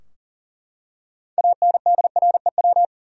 Morse code, heard as a keyed tone, about 40 wpm, 700 Hz. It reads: ANDREW
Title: USA Common Names - 40wpm